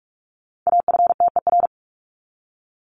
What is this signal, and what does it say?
Morse code, heard as a keyed tone, about 44 wpm, 700 Hz. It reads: AFTER
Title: Top 200 Words 45wpm